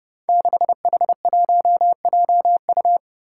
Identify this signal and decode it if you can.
Morse code, heard as a keyed tone, about 29 wpm, 700 Hz. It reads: 6H1JU
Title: Five Challenging Letter-Number Combinations 30wpm